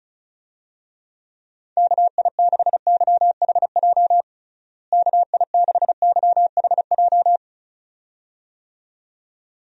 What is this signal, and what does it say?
Morse code, heard as a keyed tone, about 36 wpm, 700 Hz. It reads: KI6YHJ KI6YHJ
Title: USA Call Signs - 6 Characters - Repeated 2x - 35wpm